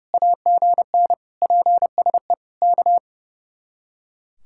Morse code, heard as a keyed tone, about 35 wpm, 700 Hz. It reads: AGN PSE K
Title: Speed-Racing Common QSO Phrases Encore - 30wpm